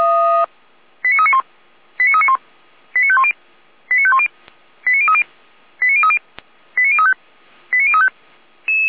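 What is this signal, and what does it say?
Five-tone selective call (ZVEI), heard as quick runs of five tones, tones of about 80 ms. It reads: 89391 89391 89510 89510 89030 89030 89035 89035
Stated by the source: ZVEI five-tone selective call 89125 received via SDR.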